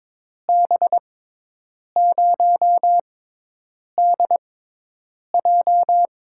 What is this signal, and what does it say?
Morse code, heard as a keyed tone, about 22 wpm, spaced as for 8 wpm, 700 Hz. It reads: B0DJ
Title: USA Call Signs - 6 Characters - 6x Character Spacing - 22wpm